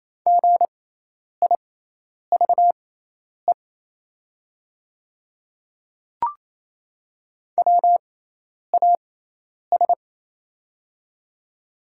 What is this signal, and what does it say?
Morse code, heard as a keyed tone, about 28 wpm, spaced as for 10 wpm, 700 Hz. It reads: GIVE WAS
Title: 6x Character Spacing - Top 200 Words - 28wpm